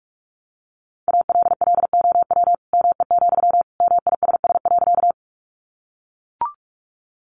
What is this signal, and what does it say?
Morse code, heard as a keyed tone, about 44 wpm, 700 Hz. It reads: ALLOW ME, MISS.
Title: Sentences from Top 700 Words 45wpm